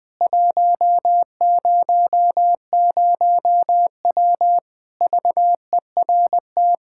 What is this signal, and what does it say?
Morse code, heard as a keyed tone, about 20 wpm, 700 Hz. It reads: 100W VERT
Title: K1USN SST QSO Practice 20wpm